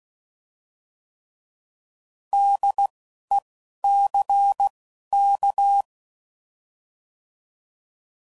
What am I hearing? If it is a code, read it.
Morse code, heard as a keyed tone, about 16 wpm, spaced as for 11 wpm, 800 Hz. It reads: DECK
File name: Prefixes / By First Letter / Ds